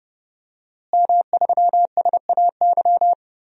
Morse code, heard as a keyed tone, about 30 wpm, 700 Hz. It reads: M3SAY
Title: DXCC-100 Call Signs - 5 Characters - Repeated 3x - 30wpm